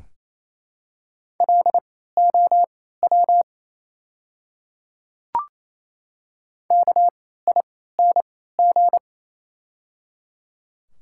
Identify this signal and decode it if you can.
Morse code, heard as a keyed tone, about 28 wpm, spaced as for 16 wpm, 700 Hz. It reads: LOW KING
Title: 3x Character Spacing - Top 500 Words - 28wpm